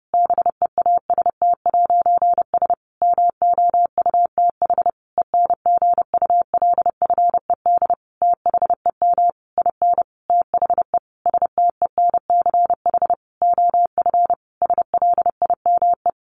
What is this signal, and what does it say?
Morse code, heard as a keyed tone, about 30 wpm, 700 Hz. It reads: BEAST'S MOUTH ENGULFED THEM IN THE STENCH OF SLIME
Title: A Sound Of Thunder 30wpm